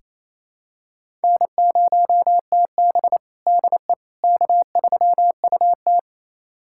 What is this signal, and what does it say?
Morse code, heard as a keyed tone, about 28 wpm, 700 Hz. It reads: N0TB DE K3UT
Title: USA Callsign DE Callsign - 4 Characters - 28wpm